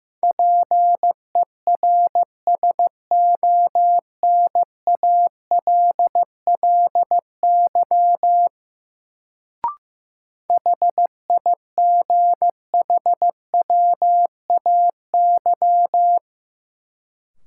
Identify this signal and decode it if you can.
Morse code, heard as a keyed tone, about 15 wpm, 700 Hz. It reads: PERSONALLY HIGHWAY
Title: Sentences from Top 1400 Words - New Word Drill - 15wpm